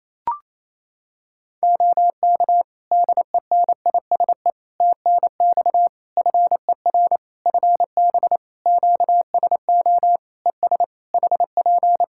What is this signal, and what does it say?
Morse code, heard as a keyed tone, about 27 wpm, 700 Hz. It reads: OK DENISE TNX FER FB QSO ES HP
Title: Practice QSO - Exchange 4 - Ending - 28wpm